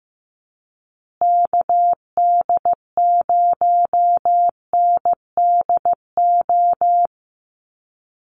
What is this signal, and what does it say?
Morse code, heard as a keyed tone, about 15 wpm, 700 Hz. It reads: KD0NDO
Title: USA Call Signs Repeated 3x 15wpm